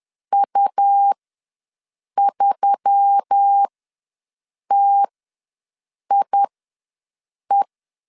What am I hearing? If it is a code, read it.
Morse code, heard as a keyed tone, about 10 wpm, spaced as for 6 wpm, 800 Hz. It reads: U3TIE
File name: LEZIONI / 5 LBVC / 5 L ws 10wpm 3636